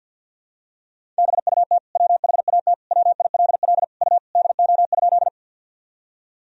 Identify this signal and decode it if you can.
Morse code, heard as a keyed tone, about 47 wpm, 700 Hz. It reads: BUT WHAT WILL U DO?
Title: Sentences from Top 100 Words and Abbreviations 50wpm